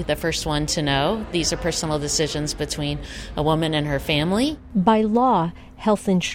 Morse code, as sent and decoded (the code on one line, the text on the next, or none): none